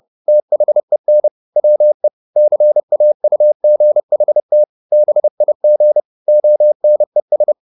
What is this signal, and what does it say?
Morse code, heard as a keyed tone, about 30 wpm, 600 Hz. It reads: THEN WE CAUGHT BIG ONES
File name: the old man and the sea / the old man and the sea 1 30wpm